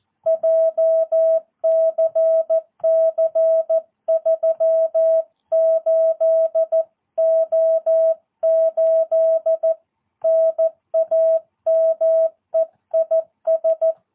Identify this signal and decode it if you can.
Morse code, heard as a keyed tone, about 14 wpm, 650 Hz. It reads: JCC 38O8 NAME IS
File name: jh8cbh / 2022cw